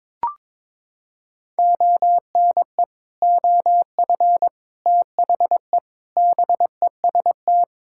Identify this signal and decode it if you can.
Morse code, heard as a keyed tone, about 22 wpm, 700 Hz. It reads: ONE OF THE BEST
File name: practice / 22wpm / Sets of 8 Words - 22wpm